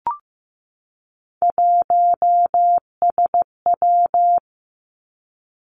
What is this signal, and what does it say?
Morse code, heard as a keyed tone, about 15 wpm, 700 Hz. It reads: 1SW
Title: Three Challenging Letter-Number Combinations 15wpm